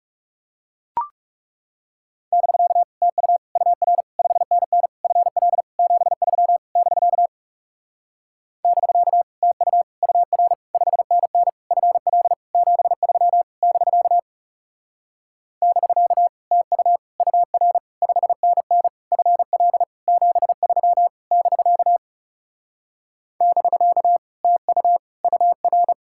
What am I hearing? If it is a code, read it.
Morse code, heard as a keyed tone, about 40 wpm, 700 Hz. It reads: <BK> TU UR 5NN FL 73 <BK> <BK> TU UR 5NN FL 73 <BK> <BK> TU UR 5NN FL 73 <BK> <BK> TU UR